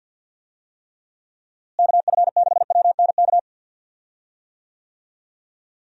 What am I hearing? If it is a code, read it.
Morse code, heard as a keyed tone, about 47 wpm, 700 Hz. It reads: KF6WNK
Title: USA Call Signs - 6 Characters - Repeated 3x - 50wpm